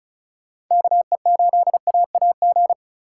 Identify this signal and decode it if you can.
Morse code, heard as a keyed tone, about 36 wpm, 700 Hz. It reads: KE8AAG
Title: Speed Racing - USA Call Signs 30wpm